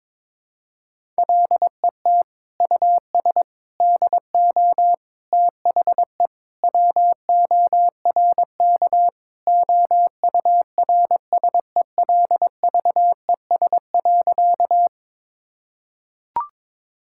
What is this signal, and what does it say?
Morse code, heard as a keyed tone, about 22 wpm, 700 Hz. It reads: LET US DO THE WORK OURSELVES.
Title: Sentences from Top 1600 Words 22wpm